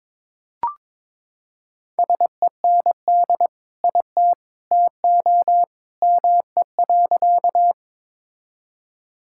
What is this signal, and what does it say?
Morse code, heard as a keyed tone, about 22 wpm, 700 Hz. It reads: SEND IT TO ME.
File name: practice / 22wpm / Sentences with 2-Word Phrases - 22wpm